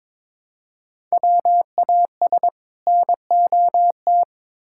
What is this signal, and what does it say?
Morse code, heard as a keyed tone, about 23 wpm, 700 Hz. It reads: WAS NOT
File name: practice / 22wpm / Build A Phrase - 2-Word Phrases - 22wpm